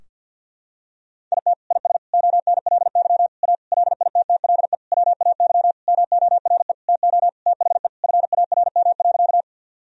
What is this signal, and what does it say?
Morse code, heard as a keyed tone, about 49 wpm, 700 Hz. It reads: IT IS ONLY A LITTLE WAY MORE TO THE FARM.
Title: Sentences from Top 1000 Words - Review Part 2 - 50wpm